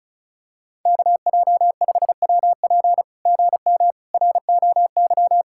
Morse code, heard as a keyed tone, about 36 wpm, 700 Hz. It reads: KJ5WP GM ROY
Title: SOTA QSO Practice 35wpm